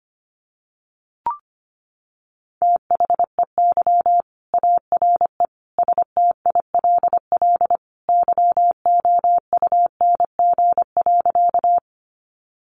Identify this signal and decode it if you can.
Morse code, heard as a keyed tone, about 25 wpm, 700 Hz. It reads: THEY ARE STILL YOUNG.
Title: Sentences from Top 400 Words 25wpm